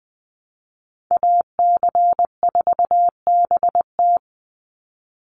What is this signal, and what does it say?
Morse code, heard as a keyed tone, about 20 wpm, 700 Hz. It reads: AC4BT
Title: Call Signs Repeated 3x - 20wpm